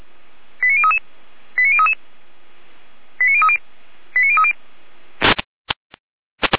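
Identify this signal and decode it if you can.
Five-tone selective call (ZVEI), heard as quick runs of five tones, tones of about 70 ms. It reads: 89033 89033 89030 89030
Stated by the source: ZVEI five-tone selective call 89010, received by SDR, from Feuerwehr Orlamünde fire brigade alerting archive.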